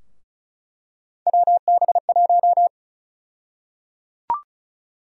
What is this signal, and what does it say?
Morse code, heard as a keyed tone, about 36 wpm, 700 Hz. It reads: WB1
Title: USA Call Sign Prefixes - Repeated 2x - 35wpm